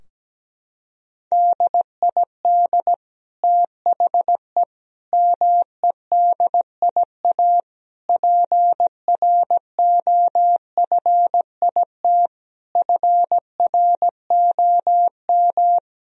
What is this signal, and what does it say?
Morse code, heard as a keyed tone, about 17 wpm, 700 Hz. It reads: DID THE MEDIA PROFIT FROM
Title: Sentences from Top 1000 Words - Review Part 4 - 17wpm